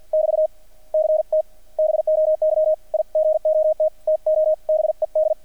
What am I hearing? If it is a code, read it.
Morse code, heard as a keyed tone, about 50 wpm, 620 Hz, with radio noise. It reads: <BT> CT BOY I GOT TO BED
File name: archived / CRSnet